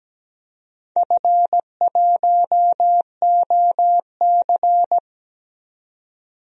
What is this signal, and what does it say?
Morse code, heard as a keyed tone, about 17 wpm, 700 Hz. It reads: F1OC
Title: DXCC-100 Call Signs - 4 Characters - 17wpm